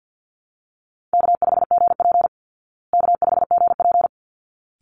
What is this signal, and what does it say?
Morse code, heard as a keyed tone, about 48 wpm, 700 Hz. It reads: K5GP K5GP
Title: Call Signs Repeated 3x - 50wpm